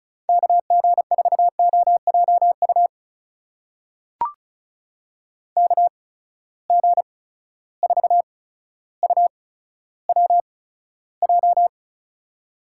Morse code, heard as a keyed tone, about 36 wpm, spaced as for 15 wpm, 700 Hz. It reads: KG4OJU KG4UWJ
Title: USA Call Signs - 6 Characters - 8x Character Spacing - 35wpm